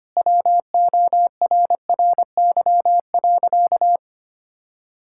Morse code, heard as a keyed tone, about 25 wpm, 700 Hz. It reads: WORRY.
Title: Pangram Challenge v2 - 25wpm